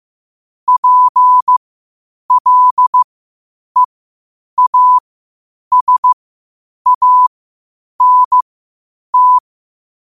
Morse code, heard as a keyed tone, about 15 wpm, spaced as for 8 wpm, 1000 Hz. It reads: PLEASANT
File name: qotdfiles / 8wpm